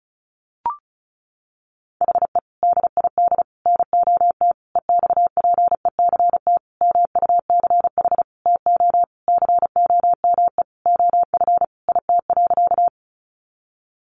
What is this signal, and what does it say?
Morse code, heard as a keyed tone, about 34 wpm, 700 Hz. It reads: HE DID NOT EXPECT MUCH TO COME OF IT.
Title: Sentences from Top 1100 Words 35wpm